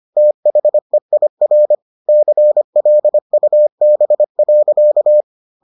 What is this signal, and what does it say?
Morse code, heard as a keyed tone, about 25 wpm, 600 Hz. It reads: THEIR CLUB.